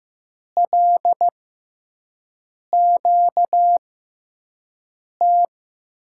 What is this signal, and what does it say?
Morse code, heard as a keyed tone, about 15 wpm, spaced as for 5 wpm, 700 Hz. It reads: LQT
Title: USA Call Signs - 5 Characters - 6x Character Spacing - 15wpm